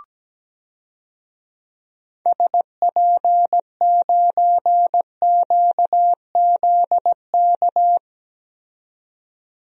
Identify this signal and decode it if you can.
Morse code, heard as a keyed tone, about 17 wpm, 700 Hz. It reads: SP9QZK
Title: DXCC-100 Call Signs - 6 Characters - 17wpm